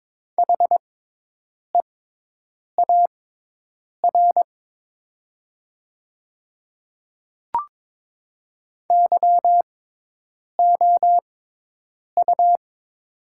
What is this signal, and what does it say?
Morse code, heard as a keyed tone, about 22 wpm, spaced as for 8 wpm, 700 Hz. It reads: HEAR YOU
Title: 6x Character Spacing - Top 500 Words - 22wpm